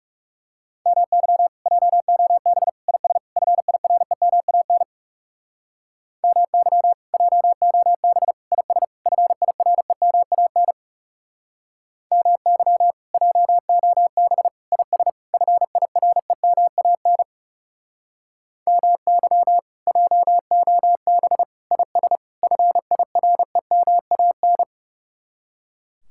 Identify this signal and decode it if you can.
Morse code, heard as a keyed tone, about 40 wpm, 700 Hz. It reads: MY JOB IS FIREMAN MY JOB IS FIREMAN MY JOB IS FIREMAN MY JOB IS FIREMAN